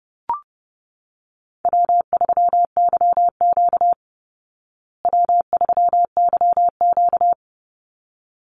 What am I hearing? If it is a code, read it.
Morse code, heard as a keyed tone, about 30 wpm, 700 Hz. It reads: W3YQ W3YQ
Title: Easy Call Signs Repeated 2x - 30wpm